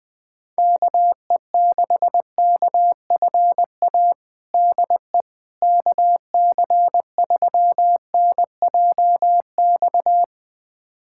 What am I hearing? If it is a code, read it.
Morse code, heard as a keyed tone, about 20 wpm, 700 Hz. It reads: KE6KFA DE KC3NJX
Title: USA Callsign DE Callsign - 6 Characters - 20wpm